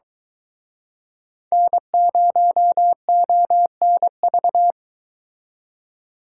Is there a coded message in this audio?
Yes: Morse code from 1.5 to 4.7 s, heard as a keyed tone, about 24 wpm, 700 Hz. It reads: N0ONV